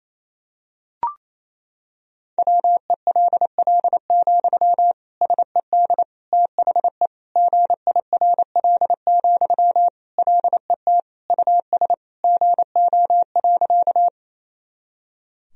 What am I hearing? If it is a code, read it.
Morse code, heard as a keyed tone, about 28 wpm, 700 Hz. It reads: WELL, SED THE GIRL, LET US GO.
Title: Sentences from Top 400 Words and Abbreviations 28wpm